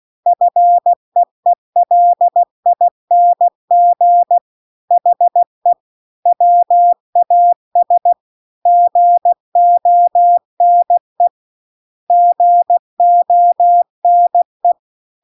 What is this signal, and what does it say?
Morse code, heard as a keyed tone, about 16 wpm, 700 Hz. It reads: FEELING HE WAS GONE GONE
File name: CW-The-Big-Twoheaded-River-2-16WPM